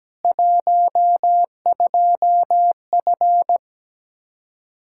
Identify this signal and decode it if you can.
Morse code, heard as a keyed tone, about 17 wpm, 700 Hz. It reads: 12F
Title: Practice QSO - Exchange 2 - Info - 17wpm